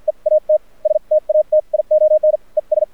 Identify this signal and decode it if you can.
Morse code, heard as a keyed tone, about 50 wpm, spaced as for 41 wpm, 610 Hz, with radio noise. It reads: E AT STATION ES
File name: archived / CRSnet